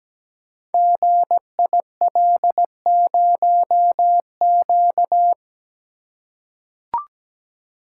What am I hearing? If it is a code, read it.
Morse code, heard as a keyed tone, about 17 wpm, 700 Hz. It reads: GIL0Q
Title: Code Groups - 5 Letter:Numbers - 17wpm